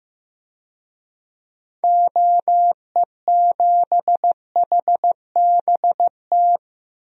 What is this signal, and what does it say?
Morse code, heard as a keyed tone, about 15 wpm, 700 Hz. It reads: OE7HBT
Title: DXCC-100 Call Signs - 6 Characters - 15wpm